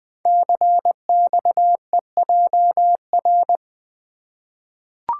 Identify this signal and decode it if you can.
Morse code, heard as a keyed tone, about 20 wpm, 700 Hz. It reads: CXEJR
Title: Code Groups - 5 Letter:Numbers - 20wpm